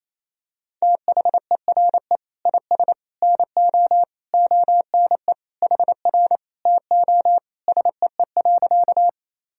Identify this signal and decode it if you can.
Morse code, heard as a keyed tone, about 28 wpm, 700 Hz. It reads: THERE IS NO ONE HR TO SEE.
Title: Sentences from Top 200 Words and Abbreviations 28wpm